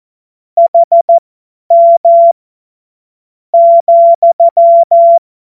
Morse code, heard as a keyed tone, about 14 wpm, spaced as for 7 wpm, 680 Hz. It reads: HM,
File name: Morsekurs / Witze / Witz 30. August 2022 50 F70